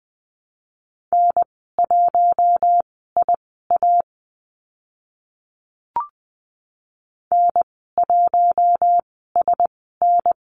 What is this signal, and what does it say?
Morse code, heard as a keyed tone, about 20 wpm, spaced as for 14 wpm, 700 Hz. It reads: N1IA N1SN
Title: 042 Lesson 8g - Number 1 - Call Signs 20wpm